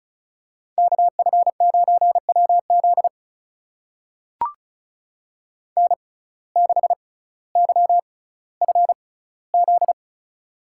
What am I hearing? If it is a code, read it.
Morse code, heard as a keyed tone, about 36 wpm, spaced as for 18 wpm, 700 Hz. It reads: KF9WZ N6YFZ
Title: USA Call Signs - 5 Characters - 6x Character Spacing - 35wpm